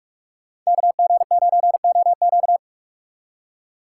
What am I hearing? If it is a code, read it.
Morse code, heard as a keyed tone, about 58 wpm, spaced as for 51 wpm, 700 Hz. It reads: KG9OQ
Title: Speed Racing - USA 2x2 Call Signs 45wpm